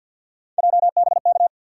Morse code, heard as a keyed tone, about 53 wpm, 700 Hz. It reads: JBK
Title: USA Call Sign Postfixes - 3 Letters 50wpm